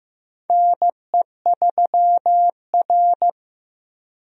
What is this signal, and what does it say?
Morse code, heard as a keyed tone, about 15 wpm, 700 Hz. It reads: NE3R
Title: USA Call Signs - 4 Characters - Repeated 2x - 15wpm